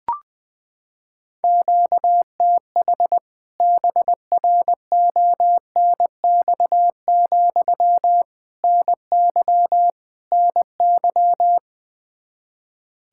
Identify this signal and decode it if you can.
Morse code, heard as a keyed tone, about 20 wpm, 700 Hz. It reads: QTH BRONX, NY NY
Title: K1USN SST QSO Practice 20wpm